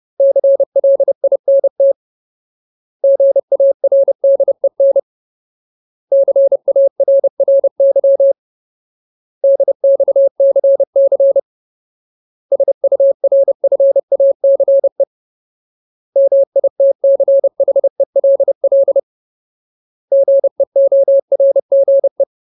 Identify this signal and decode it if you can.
Morse code, heard as a keyed tone, about 30 wpm, 550 Hz. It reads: CLINT GARDEN CARRY DXCC SURFACE MITCHELL GEORGE
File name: Ultimate Word List / 30WPM Ultimate